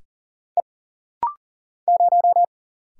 Morse code, heard as a keyed tone, about 39 wpm, 700 Hz. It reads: E 0
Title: Single Letter-Number - Rapid-Fire 40wpm